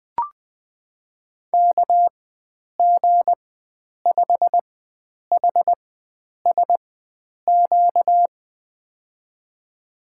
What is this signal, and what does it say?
Morse code, heard as a keyed tone, about 20 wpm, spaced as for 9 wpm, 700 Hz. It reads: KG5HSQ
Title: USA Call Signs - 6 Characters - 4x Character Spacing - 20wpm